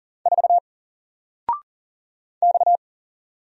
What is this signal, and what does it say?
Morse code, heard as a keyed tone, about 40 wpm, 700 Hz. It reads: 4 X
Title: Single Letter-Number 40wpm